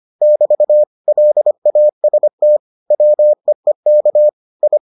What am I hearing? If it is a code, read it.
Morse code, heard as a keyed tone, about 25 wpm, 600 Hz. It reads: <BT> LAST WEEK I